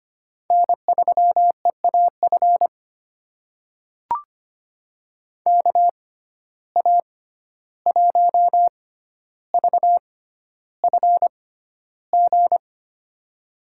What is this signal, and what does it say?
Morse code, heard as a keyed tone, about 25 wpm, spaced as for 12 wpm, 700 Hz. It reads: N3EAF KA1VFG
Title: USA Call Signs - 6x Character Spacing - 25wpm